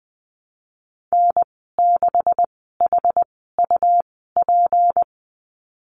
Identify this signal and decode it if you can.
Morse code, heard as a keyed tone, about 20 wpm, spaced as for 15 wpm, 700 Hz. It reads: N6HUP
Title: 154 Lesson 25g - Letter p - Call Signs 20wpm